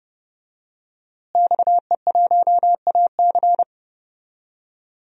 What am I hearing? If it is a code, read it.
Morse code, heard as a keyed tone, about 32 wpm, 700 Hz. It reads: XE1AC
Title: DXCC-100 Call Signs - 5 Characters - Repeated 3x - 30wpm